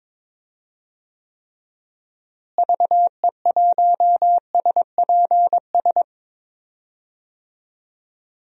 Morse code, heard as a keyed tone, about 22 wpm, 700 Hz. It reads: VE1SPS
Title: DXCC-100 Call Signs - 6 Characters - Repeated 2x - 22wpm